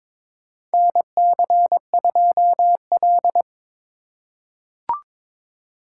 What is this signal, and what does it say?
Morse code, heard as a keyed tone, about 21 wpm, 700 Hz. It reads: NC2L
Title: USA Call Signs - 2x Character Spacing - 22wpm